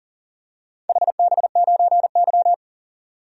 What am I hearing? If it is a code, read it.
Morse code, heard as a keyed tone, about 39 wpm, 700 Hz. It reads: HB9Y
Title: DXCC-100 Call Signs - 4 Characters - 40wpm